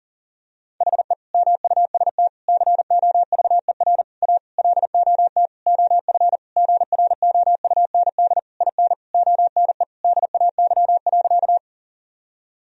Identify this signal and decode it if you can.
Morse code, heard as a keyed tone, about 40 wpm, 700 Hz. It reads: HE MUST COVER A LOT OF GROUND IN ONE DAY.
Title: Sentences from Top 1000 Words - Review Part 3 - 40wpm